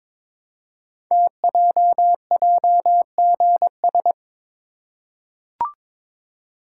Morse code, heard as a keyed tone, about 22 wpm, 700 Hz. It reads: TJJGS
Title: Code Groups - 5 Letters - 22wpm